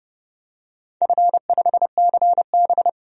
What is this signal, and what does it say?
Morse code, heard as a keyed tone, about 30 wpm, 700 Hz. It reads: F5CB
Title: DXCC-100 Call Signs - 4 Characters - 30wpm